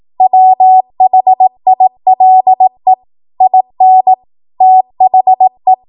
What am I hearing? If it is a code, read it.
Morse code, heard as a keyed tone, about 18 wpm, 750 Hz. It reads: WHILE IN THE
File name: Morse / 18WPM